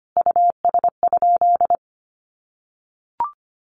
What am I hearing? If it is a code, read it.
Morse code, heard as a keyed tone, about 25 wpm, 700 Hz. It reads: US?
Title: Sentences from Top 600 Words 25wpm